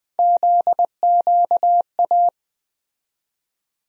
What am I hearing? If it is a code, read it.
Morse code, heard as a keyed tone, about 20 wpm, 700 Hz. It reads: ZQA
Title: USA Call Sign Postfixes - 3 Letters 20wpm